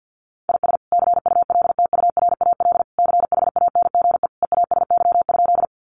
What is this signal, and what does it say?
Morse code, heard as a keyed tone, about 48 wpm, 700 Hz. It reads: IS CULTURAL CHANGE EASY?
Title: Sentences from Top 1500 Words 50wpm